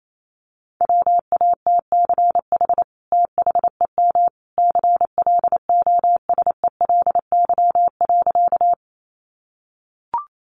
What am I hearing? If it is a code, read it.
Morse code, heard as a keyed tone, about 28 wpm, 700 Hz. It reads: WATCH THEM CLOSELY.
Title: Sentences from Top 1400 Words 28wpm